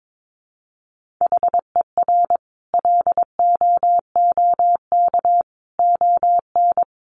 Morse code, heard as a keyed tone, about 22 wpm, 700 Hz. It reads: HER LOOK ON
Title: Sentences from Top 1400 Words 22wpm